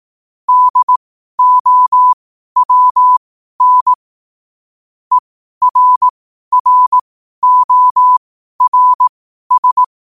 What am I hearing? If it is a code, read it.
Morse code, heard as a keyed tone, about 18 wpm, spaced as for 12 wpm, 1000 Hz. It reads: DOWN ERRORS
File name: qotdfiles / 12wpm